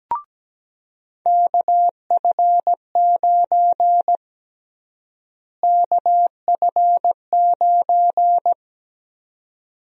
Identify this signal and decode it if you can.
Morse code, heard as a keyed tone, about 17 wpm, 700 Hz. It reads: KF9 KF9
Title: USA Call Sign Prefixes - Repeated 2x - 17wpm